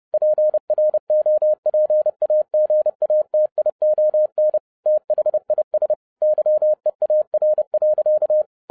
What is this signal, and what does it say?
Morse code, heard as a keyed tone, about 30 wpm, 610 Hz. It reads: PROPAGATION THIS YEAR.